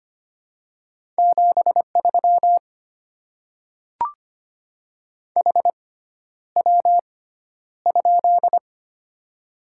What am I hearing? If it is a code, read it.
Morse code, heard as a keyed tone, about 25 wpm, spaced as for 11 wpm, 700 Hz. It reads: 73 HW?
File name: Top 20 QSO Elements - 6x Character Spacing - 25wpm